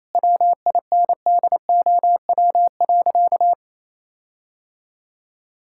Morse code, heard as a keyed tone, about 28 wpm, 700 Hz. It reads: WINDOW.
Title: Sentences from Top 1600 Words 28wpm